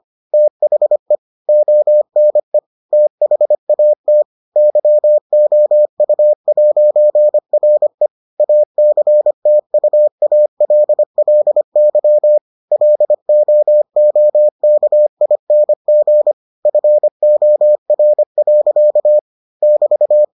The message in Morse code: THE ONE THAT YOU'RE ACTUALLY LOOKING FOR. <BT>

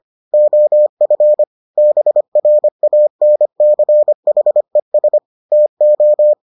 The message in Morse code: OF BRANCHES TO

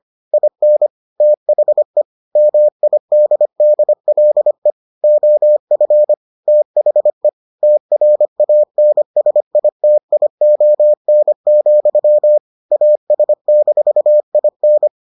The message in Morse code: IN THE MIDDLE OF THE TRANSITION, AS-IN

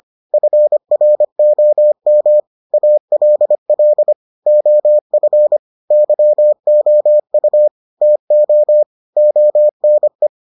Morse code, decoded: FROM ALL OF YOU TO ONE

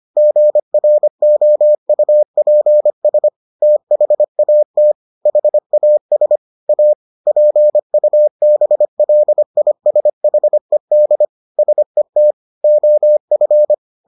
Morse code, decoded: GROUPS THAT HAS A PUBLISHED SET OF

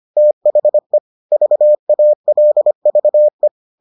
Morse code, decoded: THE VALVE

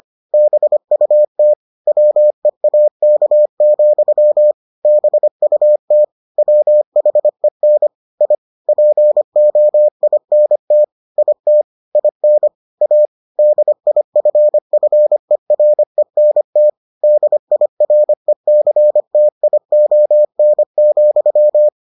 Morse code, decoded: BUT WEAK, BUT WHEN I POINT IT IN A DIFFERENT DIRECTION,

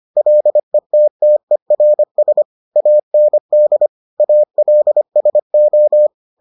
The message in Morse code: LETTERS AND ALSO

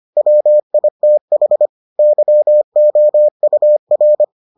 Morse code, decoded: WITH YOUR

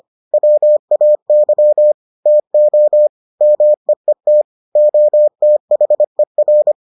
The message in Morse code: WAY TO MEET OTHER

